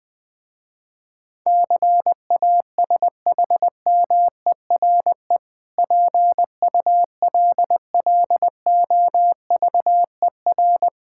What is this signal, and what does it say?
Morse code, heard as a keyed tone, about 20 wpm, 700 Hz. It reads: CASHMERE PULLOVER
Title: Difficult Words with Hints - 20wpm